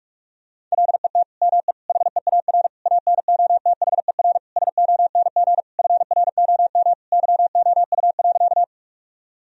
Morse code, heard as a keyed tone, about 46 wpm, 700 Hz. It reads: LET ME HEAR ANOTHER SONG FROM YOU.
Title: Sentences from Top 1000 Words - Review Part 2 - 45wpm